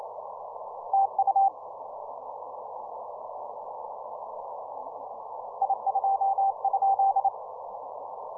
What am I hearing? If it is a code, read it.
Morse code, heard as a keyed tone, about 28 wpm, 780 Hz. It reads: TU I 2?